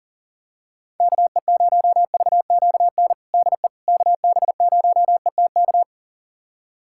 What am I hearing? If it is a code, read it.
Morse code, heard as a keyed tone, about 40 wpm, 700 Hz. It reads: KE0VQN DE KB0ETK
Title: USA Callsign DE Callsign 40wpm